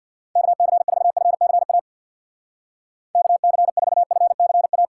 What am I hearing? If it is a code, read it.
Morse code, heard as a keyed tone, about 53 wpm, 700 Hz. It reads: KC4FCA KC4FCA
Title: Speed Racing - USA 2x3 Call Signs 50wpm